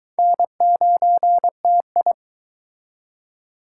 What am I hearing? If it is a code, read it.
Morse code, heard as a keyed tone, about 22 wpm, 700 Hz. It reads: N9TI